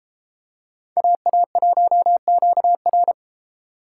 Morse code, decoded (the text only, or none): AA1QR